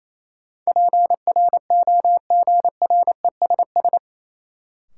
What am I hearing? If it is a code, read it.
Morse code, heard as a keyed tone, about 28 wpm, 700 Hz. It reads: PROGRESS
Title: Sentences from Top 1400 Words - New Word Drill Alt - 28wpm